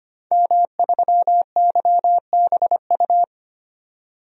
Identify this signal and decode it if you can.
Morse code, heard as a keyed tone, about 25 wpm, 700 Hz. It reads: M3YBU
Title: DXCC-100 Call Signs - 5 Characters - 25wpm